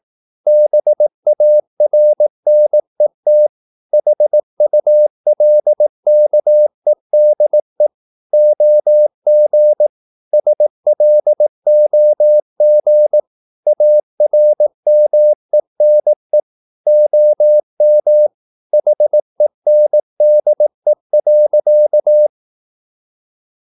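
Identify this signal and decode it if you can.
Morse code, heard as a keyed tone, about 18 wpm, 600 Hz. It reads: BARNET HULKEDE OG SLOG ARMENE OM HENDE.
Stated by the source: Slow (QRS) Morse practice at 18 wpm, CW med Gnister.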